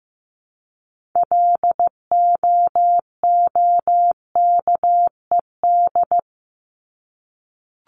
Morse code, heard as a keyed tone, about 15 wpm, 700 Hz. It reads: LOOKED
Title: ED Suffix - Top 100 Words with Base Words 15wpm